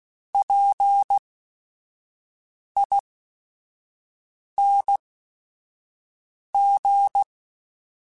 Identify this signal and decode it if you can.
Morse code, heard as a keyed tone, about 16 wpm, spaced as for 5 wpm, 800 Hz. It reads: PING
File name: Special INGs / Special INGs-mp3s / Sngl Ing @5